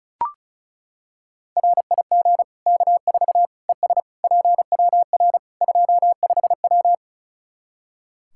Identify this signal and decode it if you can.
Morse code, heard as a keyed tone, about 36 wpm, 700 Hz. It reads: RIG K4 ES PWR 25W
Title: Practice QSO - Exchange 2 - Info - 35wpm